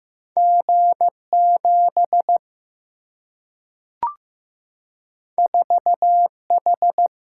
Morse code, heard as a keyed tone, about 15 wpm, 700 Hz. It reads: G 7 4 H
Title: Single-Letter Single-Number 15wpm